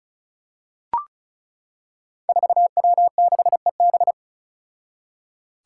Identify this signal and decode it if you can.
Morse code, heard as a keyed tone, about 36 wpm, 700 Hz. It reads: 4W6EB
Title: Five Challenging Letter-Number Combinations 35wpm